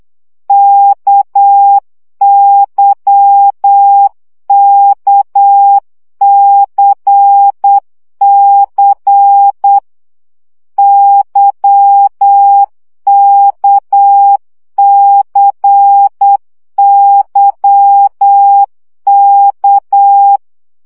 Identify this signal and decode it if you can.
Morse code, heard as a keyed tone, about 8 wpm, 800 Hz. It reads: KYKCC YKCYK
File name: CW gyakorlas k c y 7 baud